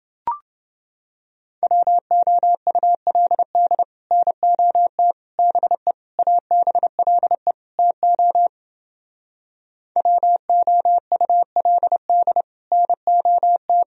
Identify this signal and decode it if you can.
Morse code, heard as a keyed tone, about 31 wpm, 700 Hz. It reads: WOULD NOT BE ABLE TO WOULD NOT